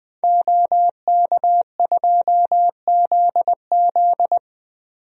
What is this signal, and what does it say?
Morse code, heard as a keyed tone, about 21 wpm, 700 Hz. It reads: OK2ZZ
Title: DXCC-100 Call Signs - 5 Characters - 20wpm